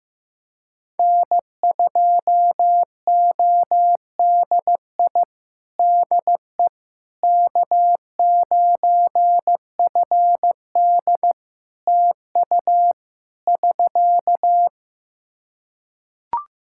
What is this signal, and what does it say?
Morse code, heard as a keyed tone, about 15 wpm, 700 Hz. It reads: N2ODI DE K9FD TU <SK>
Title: Practice QSO - Exchange 4 - Ending - 15wpm